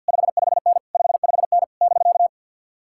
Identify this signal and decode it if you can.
Morse code, heard as a keyed tone, about 67 wpm, spaced as for 58 wpm, 700 Hz. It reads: 55N 55N <BK>
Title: Speed-Racing Common QSO Phrases Encore - 50wpm